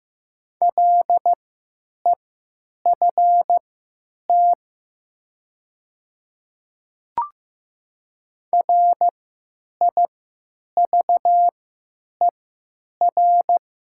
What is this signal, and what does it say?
Morse code, heard as a keyed tone, about 15 wpm, spaced as for 9 wpm, 700 Hz. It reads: LEFT RIVER
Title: 3x Character Spacing - Top 500 Words - 15wpm